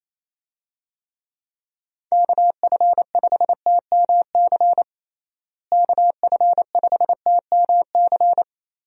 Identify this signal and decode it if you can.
Morse code, heard as a keyed tone, about 27 wpm, 700 Hz. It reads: KF5TMC KF5TMC
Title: USA Call Signs - 6 Characters - Repeated 3x - 28wpm